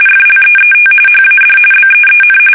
RTTY, about 45.45 baud. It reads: FLEET HAD BEEN
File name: fsk850 / usnavy5